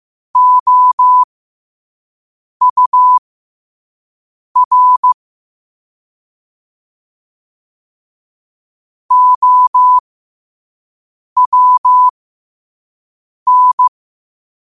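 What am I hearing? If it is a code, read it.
Morse code, heard as a keyed tone, about 15 wpm, spaced as for 5 wpm, 1000 Hz. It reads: OUR OWN
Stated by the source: Morse code at 5 wpm.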